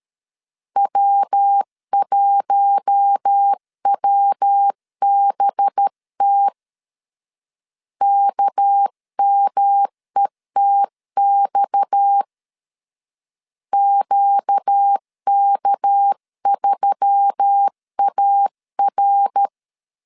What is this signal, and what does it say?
Morse code, heard as a keyed tone, about 13 wpm, 800 Hz. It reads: W1WBT KMETX QK3AR
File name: CW / DET / 1 6 1500C S C 12wpm 3533